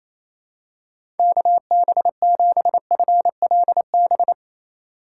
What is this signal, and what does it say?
Morse code, heard as a keyed tone, about 27 wpm, 700 Hz. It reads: KB7FLB
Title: USA Call Signs - 6 Characters - 3x Character Spacing - 28wpm